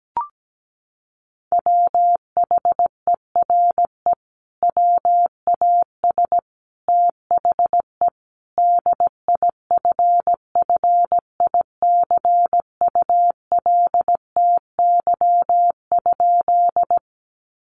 Morse code, heard as a keyed tone, about 17 wpm, 700 Hz. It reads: WHERE WAS THE DIFFICULTY?
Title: Sentences from Top 1500 Words 17wpm